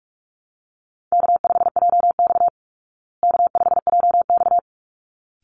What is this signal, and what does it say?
Morse code, heard as a keyed tone, about 44 wpm, 700 Hz. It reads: K5JX K5JX
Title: Easy Call Signs Repeated 2x - 45wpm